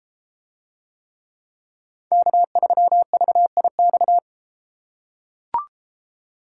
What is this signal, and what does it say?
Morse code, heard as a keyed tone, about 32 wpm, 700 Hz. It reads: K3VIX